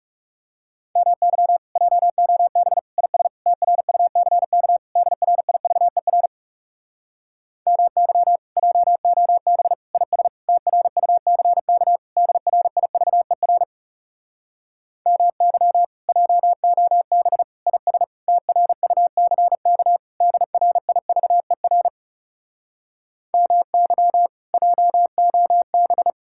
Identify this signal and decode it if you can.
Morse code, heard as a keyed tone, about 40 wpm, 700 Hz. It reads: MY JOB IS TRUCK DRIVER MY JOB IS TRUCK DRIVER MY JOB IS TRUCK DRIVER MY JOB